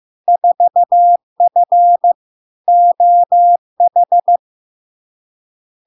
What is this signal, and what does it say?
Morse code, heard as a keyed tone, about 15 wpm, 700 Hz. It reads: 4F OH
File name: LARK-FD-2022-Log-in-CW-15wpm-700-Hz